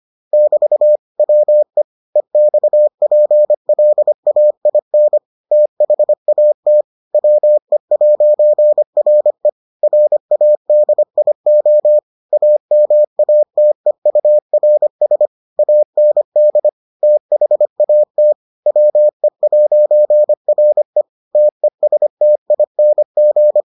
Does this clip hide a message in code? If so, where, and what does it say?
Yes: Morse code from 0.3 to 23.6 s, heard as a keyed tone, about 25 wpm, 600 Hz. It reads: <BT> WE EXPLAIN THAT WE'RE RADIO AMATEURS AND THAT WE'RE TESTING